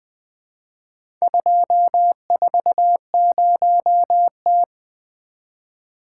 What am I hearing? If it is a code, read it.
Morse code, heard as a keyed tone, about 20 wpm, 700 Hz. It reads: 240T
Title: Four Letter-Number Combinations - 20wpm